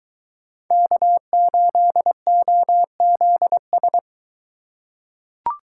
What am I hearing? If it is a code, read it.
Morse code, heard as a keyed tone, about 21 wpm, 700 Hz. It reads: K8OZS